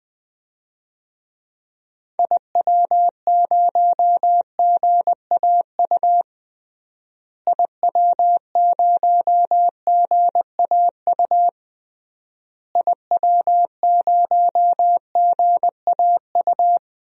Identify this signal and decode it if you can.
Morse code, heard as a keyed tone, about 20 wpm, 700 Hz. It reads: IW0GAU IW0GAU IW0GAU
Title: DXCC-100 Call Signs - 6 Characters - Repeated 3x - 20wpm